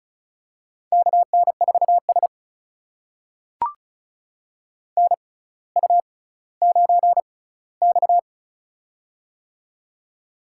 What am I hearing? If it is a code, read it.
Morse code, heard as a keyed tone, about 36 wpm, spaced as for 18 wpm, 700 Hz. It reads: KN4S NU9X
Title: USA Call Signs - 4 Characters - 6x Character Spacing - 35wpm